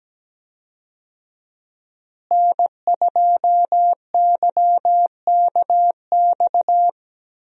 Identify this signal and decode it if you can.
Morse code, heard as a keyed tone, about 17 wpm, 700 Hz. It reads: N2YKX
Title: USA Call Signs - 5 Characters - Repeated 2x - 17wpm